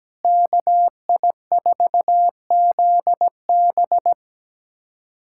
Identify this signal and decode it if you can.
Morse code, heard as a keyed tone, about 17 wpm, 700 Hz. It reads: KI4ZB
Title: USA Callsign DE Callsign - 5 Characters - 17wpm